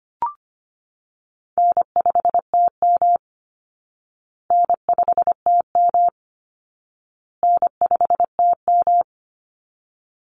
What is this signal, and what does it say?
Morse code, heard as a keyed tone, about 25 wpm, 700 Hz. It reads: N5TM N5TM N5TM
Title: Easy Call Signs Repeated 3x - 25wpm